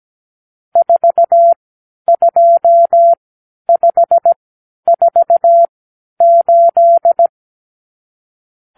Morse code, heard as a keyed tone, about 17 wpm, spaced as for 10 wpm, 680 Hz. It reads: 42548
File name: k7qo / Lesson 54 (9)